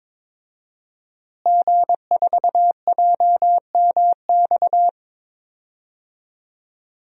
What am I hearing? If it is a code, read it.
Morse code, heard as a keyed tone, about 22 wpm, 700 Hz. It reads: G4JMX
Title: DXCC-100 Call Signs - 5 Characters - Repeated 3x - 22wpm